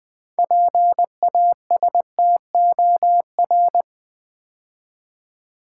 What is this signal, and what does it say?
Morse code, heard as a keyed tone, about 20 wpm, 700 Hz. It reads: PASTOR
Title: 6 Letter Words - First 3 Letters Given - 20wpm